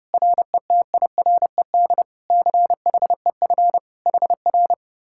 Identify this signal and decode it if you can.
Morse code, heard as a keyed tone, about 37 wpm, 700 Hz. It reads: RETIRED CHEF HR
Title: Speed-Racing Common QSO Phrases Encore - 30wpm